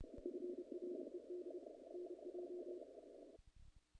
Morse code, heard as a keyed tone, about 30 wpm, 360 Hz, with noise. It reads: FRED NJ
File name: contests / naqp-cw-k5tr-op-k5pi- -1800